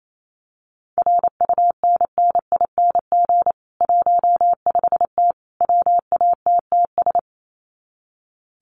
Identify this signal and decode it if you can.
Morse code, heard as a keyed tone, about 28 wpm, 700 Hz. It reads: RUNNING 15T WATTS
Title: Common QSO Phrases 28wpm